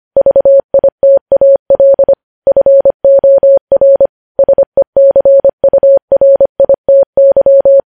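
Morse code, heard as a keyed tone, about 25 wpm, 550 Hz. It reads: VITAL FOR SECURITY
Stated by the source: Morse code news headlines at 25 wpm.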